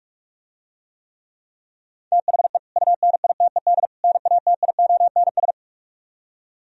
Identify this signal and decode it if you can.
Morse code, heard as a keyed tone, about 44 wpm, 700 Hz. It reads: THE UNITED NATIONS